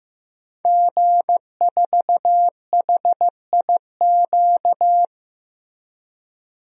Morse code, heard as a keyed tone, about 15 wpm, 700 Hz. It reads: G4HIQ
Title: DXCC-100 Call Signs - 5 Characters - 15wpm